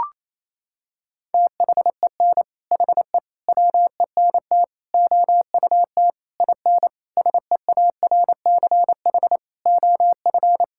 Morse code, heard as a keyed tone, about 27 wpm, 700 Hz. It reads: THEN HE WENT OUT IN SEARCH OF
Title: Sentences from Top 1000 Words - Review Part 5 - 28wpm